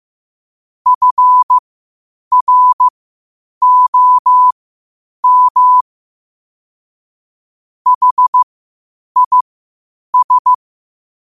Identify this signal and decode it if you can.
Morse code, heard as a keyed tone, about 15 wpm, spaced as for 8 wpm, 1000 Hz. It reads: FROM HIS
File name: qotdfiles / 8wpm